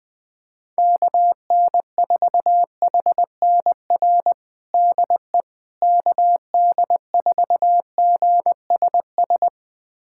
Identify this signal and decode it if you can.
Morse code, heard as a keyed tone, about 20 wpm, 700 Hz. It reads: KN4HNR DE KD4GSS
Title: USA Callsign DE Callsign - 6 Characters - 20wpm